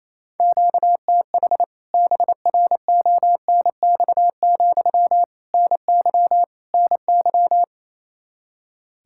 Morse code, heard as a keyed tone, about 28 wpm, 700 Hz. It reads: QTH BRONX, NY NY
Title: K1USN SST QSO Practice 28wpm